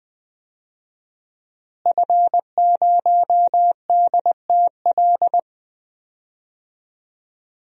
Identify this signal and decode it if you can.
Morse code, heard as a keyed tone, about 20 wpm, 700 Hz. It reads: F0DTL
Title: DXCC-100 Call Signs - 5 Characters - Repeated 3x - 20wpm